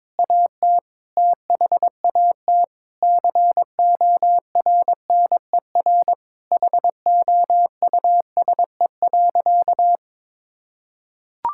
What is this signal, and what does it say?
Morse code, heard as a keyed tone, about 22 wpm, 700 Hz. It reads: AT THAT CORNER HOUSE.
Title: Sentences from Top 1600 Words 22wpm